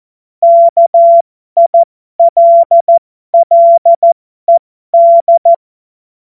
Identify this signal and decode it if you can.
Morse code, heard as a keyed tone, about 14 wpm, spaced as for 12 wpm, 680 Hz. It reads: KILLED